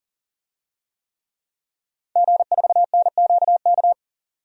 Morse code, heard as a keyed tone, about 40 wpm, 700 Hz. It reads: G4NQK
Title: DXCC-100 Call Signs - 5 Characters - 40wpm